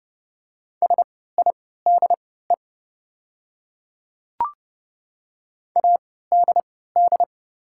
Morse code, heard as a keyed tone, about 30 wpm, spaced as for 17 wpm, 700 Hz. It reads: SIDE ADD
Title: 3x Character Spacing - Top 200 Words - 30wpm